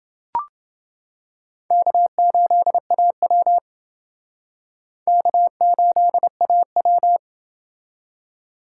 Morse code, heard as a keyed tone, about 29 wpm, 700 Hz. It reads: K8AW K8AW